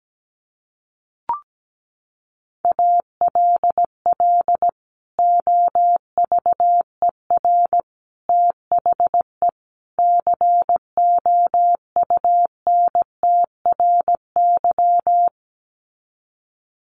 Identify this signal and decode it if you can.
Morse code, heard as a keyed tone, about 17 wpm, 700 Hz. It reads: ALL OVER THE COUNTRY
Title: Sets of 4 Words 17wpm